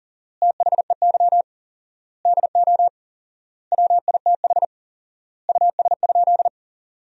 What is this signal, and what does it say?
Morse code, heard as a keyed tone, about 40 wpm, 700 Hz. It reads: THEY DO WITH US?
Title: 4x Word-Spacing - Sentences from Top 200 Words - 40wpm